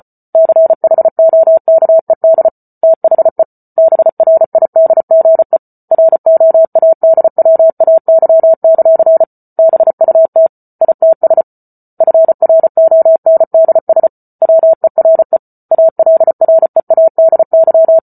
Morse code, heard as a keyed tone, about 34 wpm, 650 Hz. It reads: CHOKED THE BRIDGE ROADWAY; BUT ITS FRONDS WERE ALREADY